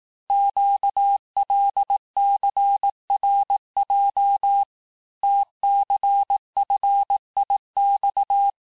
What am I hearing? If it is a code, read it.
Morse code, heard as a keyed tone, about 18 wpm, 800 Hz. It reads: QLCRJ TCFIX
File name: cours-du-lundi files / cours-1852-texte-chiffre-vitesse-17MPM